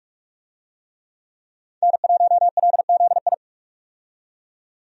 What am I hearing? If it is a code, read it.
Morse code, heard as a keyed tone, about 47 wpm, 700 Hz. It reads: N1LZI
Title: USA Call Signs - 5 Characters - 45wpm